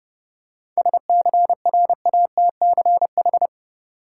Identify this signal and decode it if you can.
Morse code, heard as a keyed tone, about 30 wpm, 700 Hz. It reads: SCRATCH
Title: Seven Letter Words - 30wpm